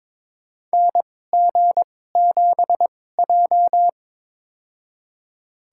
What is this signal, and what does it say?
Morse code, heard as a keyed tone, about 21 wpm, spaced as for 16 wpm, 700 Hz. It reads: NG7J
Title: USA Call Signs - 4 Characters - 2x Character Spacing - 22wpm